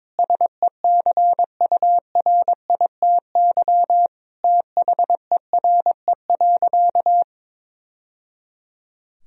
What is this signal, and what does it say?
Morse code, heard as a keyed tone, about 22 wpm, 700 Hz. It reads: SECURITY THERE.
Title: Sentences from Top 1000 Words - Review Part 4 - 22wpm